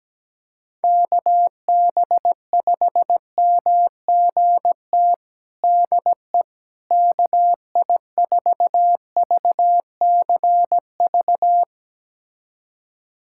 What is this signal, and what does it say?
Morse code, heard as a keyed tone, about 17 wpm, 700 Hz. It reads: KB5MGT DE KI4VCV
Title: USA Callsign DE Callsign - 6 Characters - 17wpm